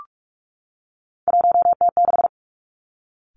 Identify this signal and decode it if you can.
Morse code, heard as a keyed tone, about 46 wpm, 700 Hz. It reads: 1T6
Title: Three Challenging Letter-Number Combinations 45wpm